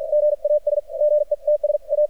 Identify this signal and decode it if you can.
Morse code, heard as a keyed tone, about 53 wpm, spaced as for 45 wpm, 600 Hz, with radio noise. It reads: GAS GETS A
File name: archived / CRSnet